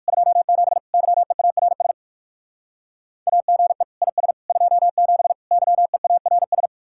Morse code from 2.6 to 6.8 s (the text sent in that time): AGE IS 27 YEARS